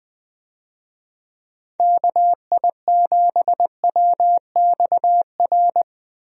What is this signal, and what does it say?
Morse code, heard as a keyed tone, about 20 wpm, 700 Hz. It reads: KI7WXR
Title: USA Call Signs - 6 Characters - Repeated 3x - 20wpm